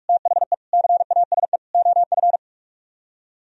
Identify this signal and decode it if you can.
Morse code, heard as a keyed tone, about 48 wpm, 700 Hz. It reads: THE CASE OF